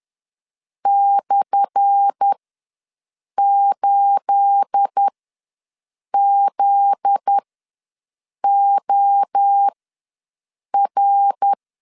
Morse code, heard as a keyed tone, about 10 wpm, spaced as for 6 wpm, 800 Hz. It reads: /8ZOR
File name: LEZIONI / 10 SIGNS / 10A SLASH ws 10wpm 4121